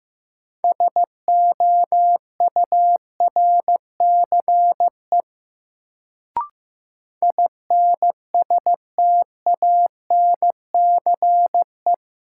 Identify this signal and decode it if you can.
Morse code, heard as a keyed tone, about 15 wpm, 700 Hz. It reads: SOURCE INSTANCE
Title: Sentences from Top 800 Words - New Word Drill - 15wpm